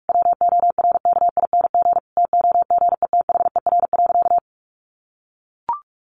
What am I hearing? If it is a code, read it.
Morse code, heard as a keyed tone, about 44 wpm, 700 Hz. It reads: WORKING TOGETHER.
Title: Sentences from Top 1200 Words 45wpm